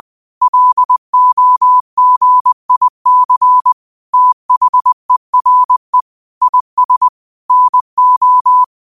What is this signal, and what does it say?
Morse code, heard as a keyed tone, about 20 wpm, 1000 Hz. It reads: LOGIC THERE IS NO
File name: qotdfiles / 20wpm